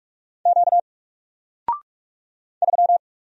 Morse code, heard as a keyed tone, about 46 wpm, 700 Hz. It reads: Q 3
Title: Single Letter-Number Alt 45wpm